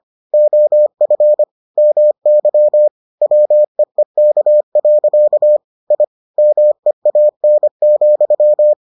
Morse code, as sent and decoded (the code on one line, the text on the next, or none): --- ..-. / -- -.-- / .-- . -.- .-.-.- / .. / -- . .- -. --..--
OF MY WEEK. I MEAN,